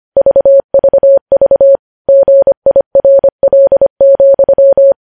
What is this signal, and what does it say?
Morse code, heard as a keyed tone, about 25 wpm, 550 Hz. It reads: VVV GIRL,
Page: News Headlines in Morse Code at 25 WPM